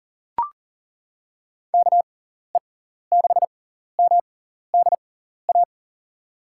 Morse code, heard as a keyed tone, about 40 wpm, spaced as for 14 wpm, 700 Hz. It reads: KE6MDA
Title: USA Call Signs - 6 Characters - 6x Character Spacing - 40wpm